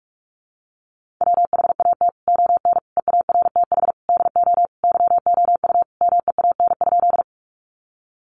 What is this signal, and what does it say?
Morse code, heard as a keyed tone, about 44 wpm, 700 Hz. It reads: WHAT ON EARTH DO YOU MEAN?
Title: Sentences from Top 200 Words 45wpm